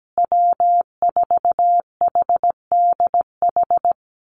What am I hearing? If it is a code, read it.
Morse code, heard as a keyed tone, about 17 wpm, 700 Hz. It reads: W4HDH
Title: USA Call Signs Repeated 2x 17wpm